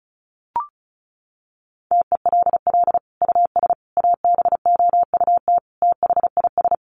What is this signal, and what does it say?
Morse code, heard as a keyed tone, about 34 wpm, 700 Hz. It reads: TELL US ABOUT THIS
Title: Sentences from Top 1200 Words 35wpm